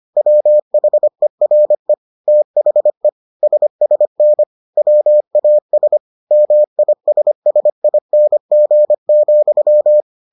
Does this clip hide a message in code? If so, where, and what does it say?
Yes: Morse code from 0.2 to 10.0 s, heard as a keyed tone, about 25 wpm, 600 Hz. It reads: WHERE THE SSN WAS MISSING,